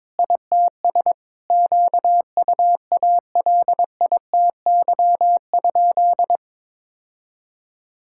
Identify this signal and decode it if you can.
Morse code, heard as a keyed tone, about 23 wpm, 700 Hz. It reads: ITS QUALITY?
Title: Sentences from Top 1000 Words - Review Part 6 - 22wpm